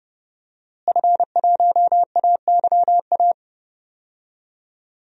Morse code, heard as a keyed tone, about 30 wpm, 700 Hz. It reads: F1AYA
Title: DXCC-100 Call Signs - 5 Characters - Repeated 2x - 30wpm